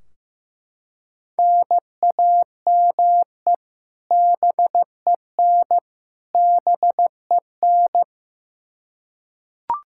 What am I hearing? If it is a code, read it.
Morse code, heard as a keyed tone, about 15 wpm, 700 Hz. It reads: NAME BEN BEN
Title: K1USN SST QSO Practice 15wpm